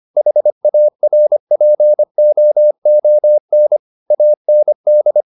Morse code, heard as a keyed tone, about 25 wpm, 600 Hz. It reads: HARPOON AND
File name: the old man and the sea part 1 25wpm